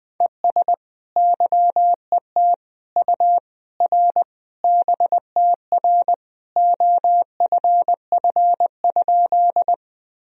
Morse code, heard as a keyed tone, about 20 wpm, 700 Hz. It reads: ES YET U R BTR OFF?
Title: Sentences from Top 500 Words and Abbreviations 20wpm